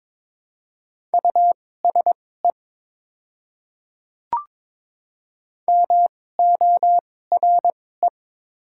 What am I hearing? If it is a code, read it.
Morse code, heard as a keyed tone, about 22 wpm, spaced as for 16 wpm, 700 Hz. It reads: USE MORE
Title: 2x Character Spacing - Top 500 Words - 22wpm